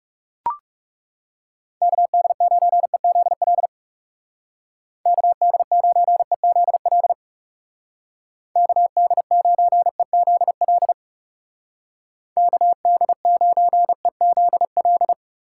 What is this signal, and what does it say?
Morse code, heard as a keyed tone, about 36 wpm, 700 Hz. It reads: KD9EZL KD9EZL KD9EZL KD9EZL
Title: Speed Racing - USA 2x3 Call Signs 30wpm